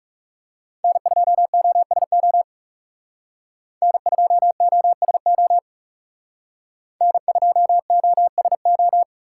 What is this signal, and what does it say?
Morse code, heard as a keyed tone, about 38 wpm, 700 Hz. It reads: N2OSO N2OSO N2OSO